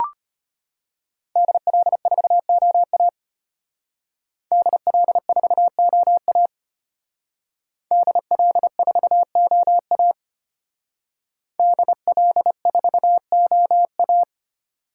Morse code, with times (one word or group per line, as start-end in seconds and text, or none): 1.4-3.1 DL4OA
4.5-6.5 DL4OA
7.9-10.1 DL4OA
11.6-14.2 DL4OA